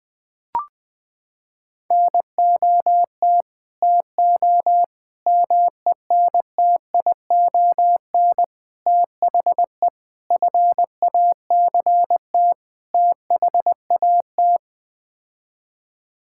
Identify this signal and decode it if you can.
Morse code, heard as a keyed tone, about 20 wpm, 700 Hz. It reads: NOT TO MENTION THE FACT THAT
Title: Sets of 6 Words - 20wpm